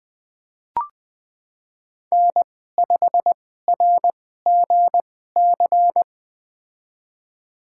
Morse code, heard as a keyed tone, about 20 wpm, spaced as for 14 wpm, 700 Hz. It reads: N5RGC
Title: USA Call Signs - 5 Characters - 2x Character Spacing - 20wpm